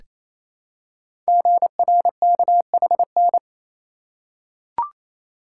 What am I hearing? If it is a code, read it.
Morse code, heard as a keyed tone, about 28 wpm, 700 Hz. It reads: GRKHN
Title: Code Groups - 5 Letter:Numbers - 28wpm